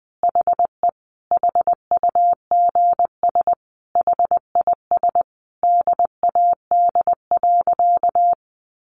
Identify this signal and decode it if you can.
Morse code, heard as a keyed tone, about 20 wpm, 700 Hz. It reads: HE HUGS HIS DAD.
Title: Easy Phrases 20wpm